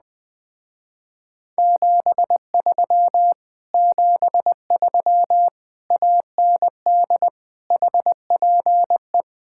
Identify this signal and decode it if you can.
Morse code, heard as a keyed tone, about 20 wpm, 700 Hz. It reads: 73 73 AND HPE
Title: Speed-Racing Common QSO Phrases - 17wpm